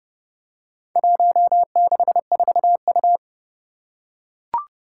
Morse code, heard as a keyed tone, about 30 wpm, 700 Hz. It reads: 164U
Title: Four Challenging Letter-Number Combinations 30wpm